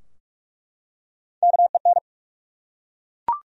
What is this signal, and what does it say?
Morse code, heard as a keyed tone, about 47 wpm, 700 Hz. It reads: KEN
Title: USA Common Names - Max 3 Letters - 45wpm